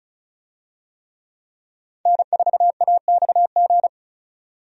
Morse code, heard as a keyed tone, about 34 wpm, 700 Hz. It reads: N4AXG
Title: USA Call Signs - 5 Characters - 35wpm